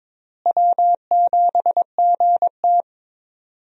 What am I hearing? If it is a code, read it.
Morse code, heard as a keyed tone, about 22 wpm, 700 Hz. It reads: W7GT
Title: USA Call Signs - 4 Characters - Repeated 3x - 22wpm